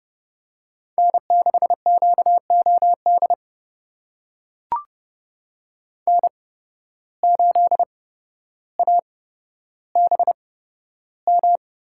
Morse code, heard as a keyed tone, about 30 wpm, spaced as for 13 wpm, 700 Hz. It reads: N6QOD N8ABM
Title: USA Call Signs - 5 Characters - 8x Character Spacing - 30wpm